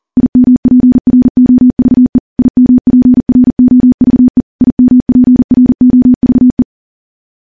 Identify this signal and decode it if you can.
Morse code, heard as a keyed tone, about 42 wpm, 250 Hz. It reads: IMPROVE IMPROVE IMPROVE
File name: morse / morse-training-40WPM